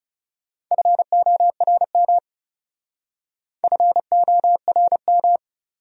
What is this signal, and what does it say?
Morse code, heard as a keyed tone, about 36 wpm, 700 Hz. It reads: FORM FORM
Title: Speed-Racing Top 200 Words 30wpm